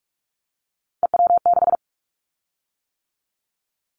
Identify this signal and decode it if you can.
Morse code, heard as a keyed tone, about 44 wpm, 700 Hz. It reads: EW6
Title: Callsign Prefixes for Top 100 DXCC - 45wpm